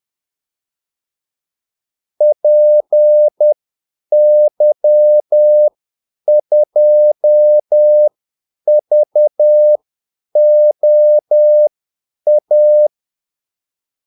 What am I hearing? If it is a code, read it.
Morse code, heard as a keyed tone, about 10 wpm, spaced as for 8 wpm, 600 Hz. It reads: PY2VOA
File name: Callsigns-10-8wpm